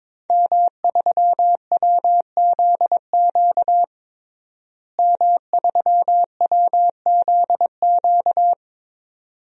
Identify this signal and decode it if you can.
Morse code, heard as a keyed tone, about 22 wpm, 700 Hz. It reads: M3WZQ M3WZQ
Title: DXCC-100 Call Signs - 5 Characters - Repeated 2x - 22wpm